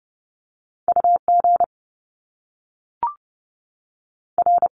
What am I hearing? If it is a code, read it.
Morse code, heard as a keyed tone, about 30 wpm, 700 Hz. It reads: U G R
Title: Top Two Letter Combinations in Words 30wpm